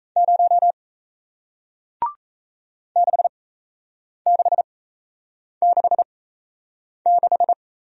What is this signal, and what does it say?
Morse code, heard as a keyed tone, about 37 wpm, spaced as for 8 wpm, 700 Hz. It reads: 0 6666